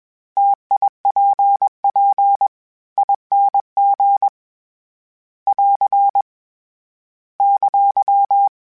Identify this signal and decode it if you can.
Morse code, heard as a keyed tone, about 21 wpm, 800 Hz. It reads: TIPPING <AR> !
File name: Variable Speed / Variable Speed-mp3s / T AOEI's sp @20